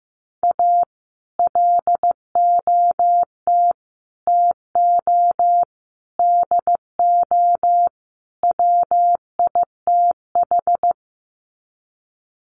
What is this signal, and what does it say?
Morse code, heard as a keyed tone, about 15 wpm, 700 Hz. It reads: A LOT TO DO WITH
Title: Sets of 5 Words 15wpm